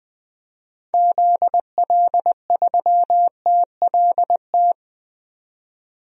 Morse code, heard as a keyed tone, about 20 wpm, 700 Hz. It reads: ZL3TLT
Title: DXCC-100 Call Signs - 6 Characters - 20wpm